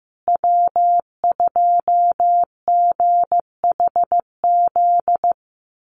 Morse code, heard as a keyed tone, about 15 wpm, 700 Hz. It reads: W2GHZ
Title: USA Call Signs Repeated 2x 15wpm